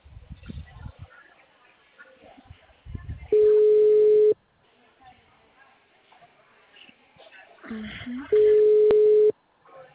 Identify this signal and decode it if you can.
Ringback tone (Europe, 425 Hz), with speech.